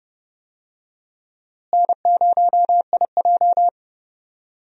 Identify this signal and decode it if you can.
Morse code, heard as a keyed tone, about 30 wpm, 700 Hz. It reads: N0IJ
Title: USA Call Signs - 4 Characters - Repeated 3x - 30wpm